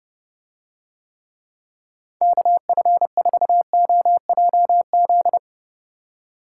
Morse code, heard as a keyed tone, about 31 wpm, 700 Hz. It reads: KF4OJZ